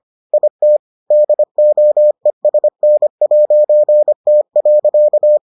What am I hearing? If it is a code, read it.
Morse code, heard as a keyed tone, about 25 wpm, 600 Hz. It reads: IT DOESN'T.